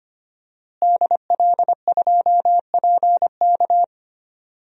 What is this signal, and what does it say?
Morse code, heard as a keyed tone, about 25 wpm, 700 Hz. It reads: DL2PK
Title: DXCC-100 Call Signs - 5 Characters - Repeated 3x - 25wpm